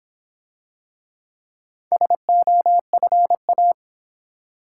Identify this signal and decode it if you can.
Morse code, heard as a keyed tone, about 26 wpm, 700 Hz. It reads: SOFA